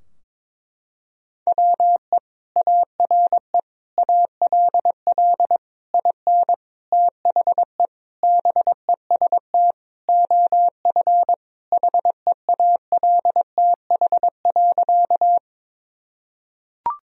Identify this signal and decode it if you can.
Morse code, heard as a keyed tone, about 22 wpm, 700 Hz. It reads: WE ARE ALL IN THE BEST OF HEALTH.
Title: Sentences from Top 1000 Words - Review Part 3 - 22wpm